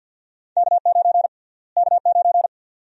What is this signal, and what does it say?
Morse code, heard as a keyed tone, about 51 wpm, 700 Hz. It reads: K9 K9
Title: USA Call Sign Prefixes - Repeated 2x - 50wpm